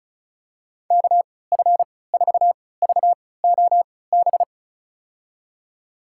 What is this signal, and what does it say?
Morse code, heard as a keyed tone, about 36 wpm, spaced as for 20 wpm, 700 Hz. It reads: KF4VOB
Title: USA Call Signs - 6 Characters - 3x Character Spacing - 35wpm